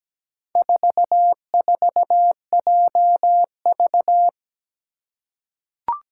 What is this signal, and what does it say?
Morse code, heard as a keyed tone, about 17 wpm, 700 Hz. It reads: 44JV
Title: Four Challenging Letter-Number Combinations 17wpm